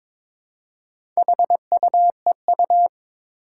Morse code, heard as a keyed tone, about 22 wpm, 700 Hz. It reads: HUEU
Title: Four Challenging Letter-Number Combinations 22wpm